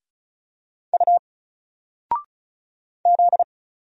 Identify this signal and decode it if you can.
Morse code, heard as a keyed tone, about 34 wpm, 700 Hz. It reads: U Z
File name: Single-Letter-Number-35wpm